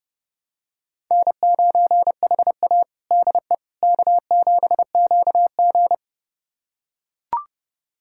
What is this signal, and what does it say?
Morse code, heard as a keyed tone, about 30 wpm, 700 Hz. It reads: N9HA DE K7QG
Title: USA Callsign DE Callsign - 4 Characters - 30wpm